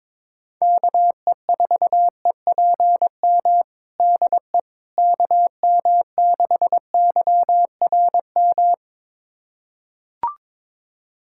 Morse code, heard as a keyed tone, about 22 wpm, 700 Hz. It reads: KE4EPM DE KM6YRM
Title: USA Callsign DE Callsign - 6 Characters - 22wpm